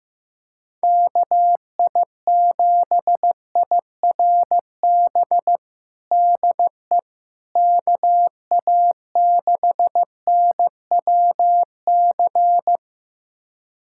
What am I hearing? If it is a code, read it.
Morse code, heard as a keyed tone, about 15 wpm, 700 Hz. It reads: KI7IRB DE KA6NWC
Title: USA Callsign DE Callsign - 6 Characters - 15wpm